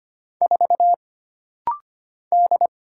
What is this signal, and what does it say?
Morse code, heard as a keyed tone, about 25 wpm, 700 Hz. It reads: 4 D
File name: Single Character (Letter:Number:Punctuation:Prosign) - Rapid-Fire - 25wpm